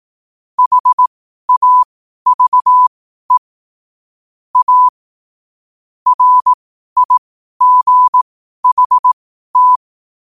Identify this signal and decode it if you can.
Morse code, heard as a keyed tone, about 18 wpm, spaced as for 12 wpm, 1000 Hz. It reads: HAVE A RIGHT
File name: qotdfiles / 12wpm